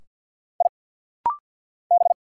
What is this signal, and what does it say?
Morse code, heard as a keyed tone, about 50 wpm, 700 Hz. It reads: I B
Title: Single Letters - Rapid-Fire 50wpm